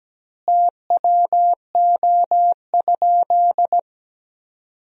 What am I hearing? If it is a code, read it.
Morse code, heard as a keyed tone, about 17 wpm, 700 Hz. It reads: TWO?
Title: 2x Word-Spacing - Sentences from Top 100 Words - 17wpm